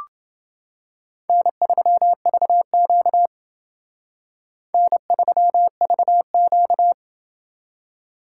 Morse code, heard as a keyed tone, about 28 wpm, 700 Hz. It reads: N3VQ N3VQ